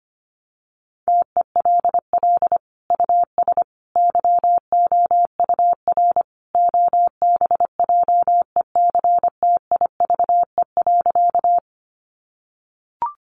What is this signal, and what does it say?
Morse code, heard as a keyed tone, about 25 wpm, 700 Hz. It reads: TELL US YOUR OBJECTIVE.
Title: Sentences from Top 1200 Words 25wpm